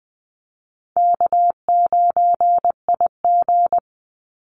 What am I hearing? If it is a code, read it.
Morse code, heard as a keyed tone, about 20 wpm, 700 Hz. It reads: K9IG
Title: Easy Call Signs Repeated 3x - 20wpm